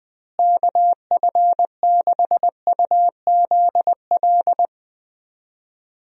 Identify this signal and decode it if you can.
Morse code, heard as a keyed tone, about 20 wpm, 700 Hz. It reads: KF6UZL
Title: USA Call Signs - 6 Characters - Repeated 2x - 20wpm